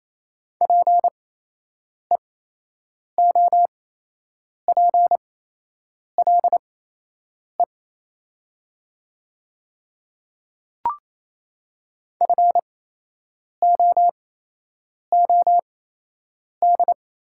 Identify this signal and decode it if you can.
Morse code, heard as a keyed tone, about 28 wpm, spaced as for 8 wpm, 700 Hz. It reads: PEOPLE FOOD
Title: 8x Character Spacing - Top 500 Words - 28wpm